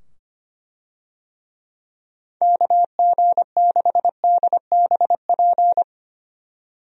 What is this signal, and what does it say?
Morse code, heard as a keyed tone, about 25 wpm, 700 Hz. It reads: KG6DBP
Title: USA Call Signs - 6 Characters - 25wpm